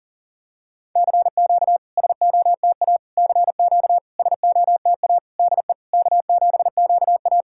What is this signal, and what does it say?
Morse code, heard as a keyed tone, about 40 wpm, 690 Hz. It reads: CQ SOTA CQ SOTA DE K7QA
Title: SOTA QSO Practice 40wpm